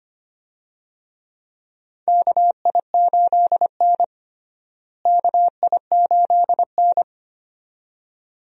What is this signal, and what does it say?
Morse code, heard as a keyed tone, about 25 wpm, 700 Hz. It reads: KI8N KI8N
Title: USA Call Signs - 4 Characters - Repeated 2x - 25wpm